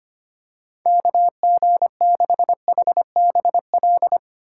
Morse code, heard as a keyed tone, about 25 wpm, 700 Hz. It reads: KG6HBL
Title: USA Call Signs - 6 Characters - 2x Character Spacing - 25wpm